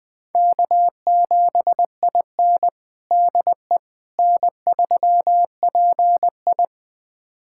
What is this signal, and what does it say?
Morse code, heard as a keyed tone, about 20 wpm, 700 Hz. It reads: K7IN DE N3PI
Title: USA Callsign DE Callsign - 4 Characters - 20wpm